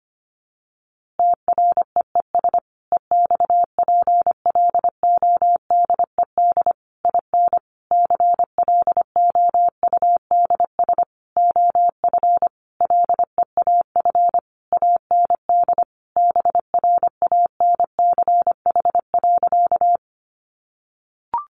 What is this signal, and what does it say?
Morse code, heard as a keyed tone, about 25 wpm, 700 Hz. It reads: TREES EXPLODED IN CLOUDS OF LEAF AND BRANCH.
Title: A Sound Of Thunder 25wpm